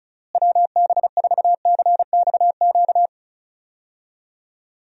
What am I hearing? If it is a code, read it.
Morse code, heard as a keyed tone, about 36 wpm, 700 Hz. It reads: WB4CXQ
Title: USA Call Signs - 6 Characters - Repeated 2x - 35wpm